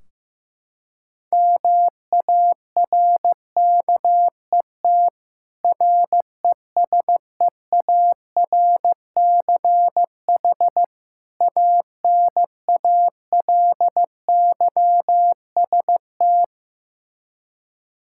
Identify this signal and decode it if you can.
Morse code, heard as a keyed tone, about 15 wpm, 700 Hz. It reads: MARKET RESEARCH ANALYST
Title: Practice QSO - Exchange 3 - More Info - 15wpm